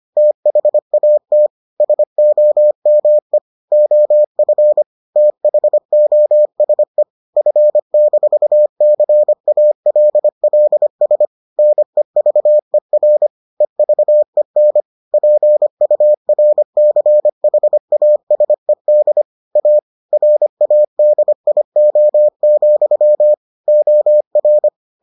Morse code, heard as a keyed tone, about 25 wpm, 600 Hz. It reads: THAT SOME OF THOSE F-CALLS NEVER EVEN PURCHASED A RADIO, OR